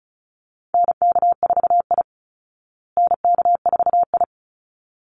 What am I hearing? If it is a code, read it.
Morse code, heard as a keyed tone, about 34 wpm, 700 Hz. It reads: NK4I NK4I
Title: Call Signs Repeated 3x - 35wpm